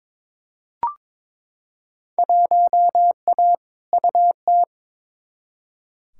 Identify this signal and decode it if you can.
Morse code, heard as a keyed tone, about 22 wpm, 700 Hz. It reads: 1A UT
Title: Field Day Exchanges - 22wpm